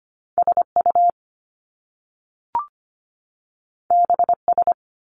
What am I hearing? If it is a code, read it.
Morse code, heard as a keyed tone, about 25 wpm, 700 Hz. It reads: SU BS
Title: Top Two Letter Combinations in Words 25wpm